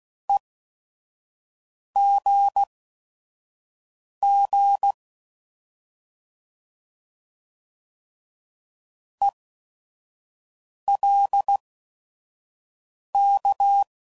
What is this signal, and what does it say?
Morse code, heard as a keyed tone, about 16 wpm, spaced as for 5 wpm, 800 Hz. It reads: EGG ELK